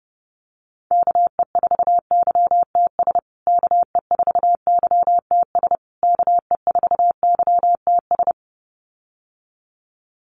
Morse code, heard as a keyed tone, about 30 wpm, 700 Hz. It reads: KE4YTS KE4YTS KE4YTS
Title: USA Call Signs Repeated 3x 30wpm